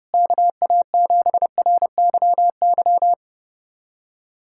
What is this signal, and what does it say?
Morse code, heard as a keyed tone, about 30 wpm, 700 Hz. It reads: KA7RYY
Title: USA Callsign DE Callsign - 6 Characters - 30wpm